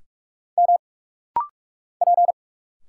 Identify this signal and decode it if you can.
Morse code, heard as a keyed tone, about 46 wpm, 700 Hz. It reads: M P
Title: Single Letters - Rapid-Fire 45wpm